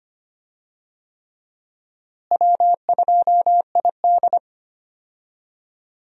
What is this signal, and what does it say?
Morse code, heard as a keyed tone, about 25 wpm, 700 Hz. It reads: W2ID
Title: USA Call Signs - 4 Characters - Repeated 3x - 25wpm